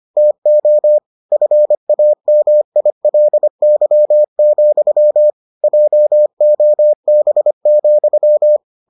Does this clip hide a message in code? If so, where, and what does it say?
Yes: Morse code from 0.2 to 8.6 s, heard as a keyed tone, about 25 wpm, 600 Hz. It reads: TO FAMILY, JOB,